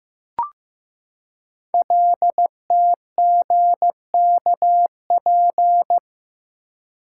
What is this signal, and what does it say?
Morse code, heard as a keyed tone, about 15 wpm, 700 Hz. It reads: LTGKP
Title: Code Groups - 5 Letter:Numbers - 15wpm